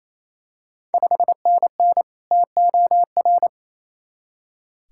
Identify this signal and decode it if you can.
Morse code, heard as a keyed tone, about 27 wpm, 690 Hz. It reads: 5NN TOR
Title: Ontario QSO Party Exchange 28wpm